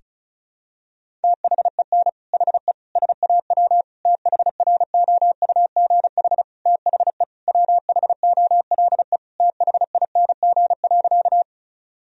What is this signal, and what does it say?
Morse code, heard as a keyed tone, about 36 wpm, 690 Hz. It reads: THEN HE SAW THROUGH THE WHOLE THING.
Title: Sentences from Top 1000 Words - Review Part 1 - 35wpm